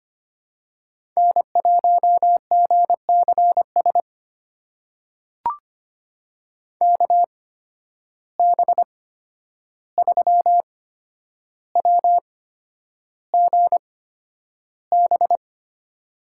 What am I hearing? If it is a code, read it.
Morse code, heard as a keyed tone, about 25 wpm, spaced as for 10 wpm, 700 Hz. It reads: N1GCS KB3WGB
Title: USA Call Signs - 8x Character Spacing - 25wpm